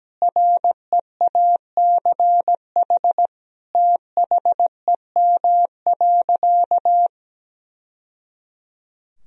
Morse code, heard as a keyed tone, about 17 wpm, 700 Hz. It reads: REACH THEM.
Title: Sentences from Top 1000 Words - Review Part 2 - 17wpm